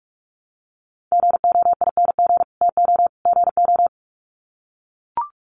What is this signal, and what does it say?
Morse code, heard as a keyed tone, about 44 wpm, 700 Hz. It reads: GOING TO GO
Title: Sets of 3 Words 45wpm